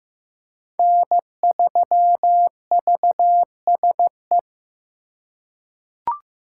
Morse code, heard as a keyed tone, about 16 wpm, 700 Hz. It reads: N3VSE
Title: USA Call Signs - 5 Characters - 3x Character Spacing - 15wpm